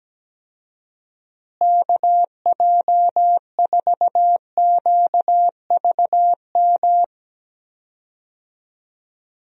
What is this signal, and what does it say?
Morse code, heard as a keyed tone, about 17 wpm, 700 Hz. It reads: KJ4QVM
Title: USA Call Signs - 6 Characters - Repeated 2x - 17wpm